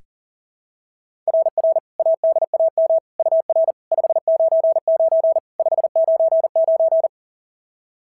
Morse code, heard as a keyed tone, about 44 wpm, 690 Hz. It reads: RR ADAM UR 599 599
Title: SOTA QSO Practice 40wpm